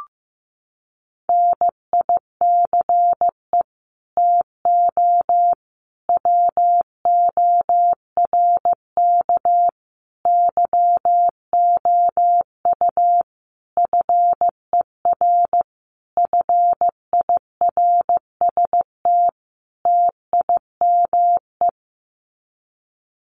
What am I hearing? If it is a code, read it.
Morse code, heard as a keyed tone, about 15 wpm, 700 Hz. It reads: NICE TO WORK YOU FER FIRST TIME
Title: Common QSO Phrases 15wpm